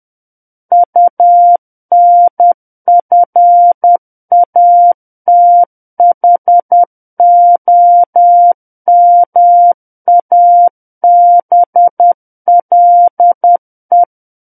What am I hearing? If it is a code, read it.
Morse code, heard as a keyed tone, about 10 wpm, 700 Hz. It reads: UNFATHOMABLE
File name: Book One / 01-Chapter 1 - 10 WPM